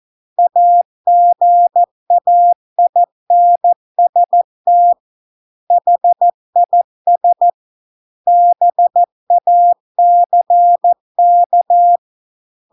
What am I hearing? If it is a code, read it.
Morse code, heard as a keyed tone, about 14 wpm, 700 Hz. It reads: AGAINST HIS BACK